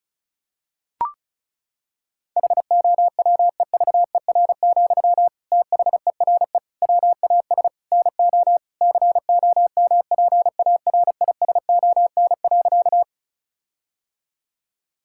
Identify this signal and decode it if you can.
Morse code, heard as a keyed tone, about 34 wpm, 700 Hz. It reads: HOWEVER, THERE WAS NO COMPARISON.
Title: Sentences from Top 1600 Words 35wpm